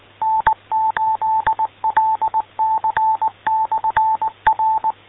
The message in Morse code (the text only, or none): N8LC/R